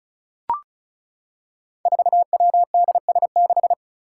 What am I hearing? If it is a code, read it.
Morse code, heard as a keyed tone, about 34 wpm, 700 Hz. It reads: 4WDS6
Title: Five Challenging Letter-Number Combinations 35wpm